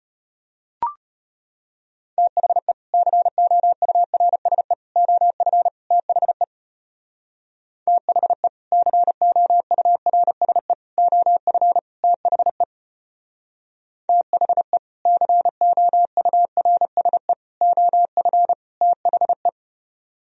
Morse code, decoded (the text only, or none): THE COURSE OF THE THE COURSE OF THE THE COURSE OF THE